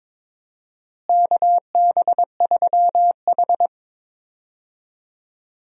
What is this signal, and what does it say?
Morse code, heard as a keyed tone, about 22 wpm, 700 Hz. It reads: KB3H
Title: USA Call Signs - 4 Characters - Repeated 3x - 22wpm